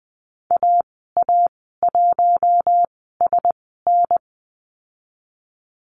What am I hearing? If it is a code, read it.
Morse code, heard as a keyed tone, about 20 wpm, spaced as for 14 wpm, 700 Hz. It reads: AA1SN
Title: 055 Lesson 10g - Letter r - Call Signs 20wpm